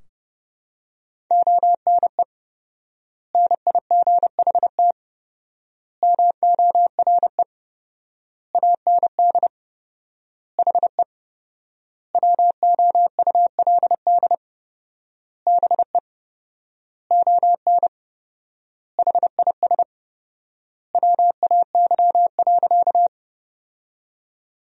Morse code, heard as a keyed tone, about 32 wpm, 700 Hz. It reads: ONE NIGHT MORE AND HE WOULD BE ON HIS WAY.
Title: 4x Word-Spacing - Sentences from Top 300 Words - 30wpm